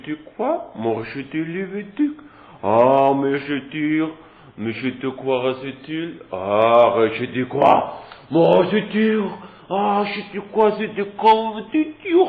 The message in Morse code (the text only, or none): none